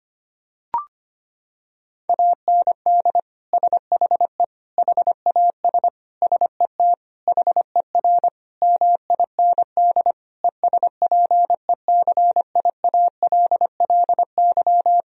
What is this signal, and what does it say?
Morse code, heard as a keyed tone, about 25 wpm, 700 Hz. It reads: AND SHE HAS SET HER MIND ESPECIALLY
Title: Sentences from Top 1000 Words - Review Part 4 - 25wpm